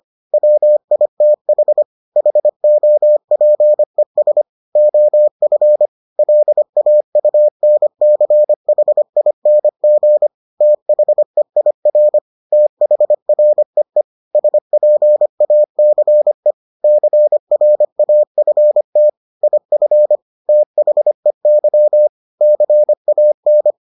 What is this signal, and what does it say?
Morse code, heard as a keyed tone, about 25 wpm, 600 Hz. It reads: WITH HOPES OF LAUNCHING THEIR THREE SPACE CRAFT IF THEY CAN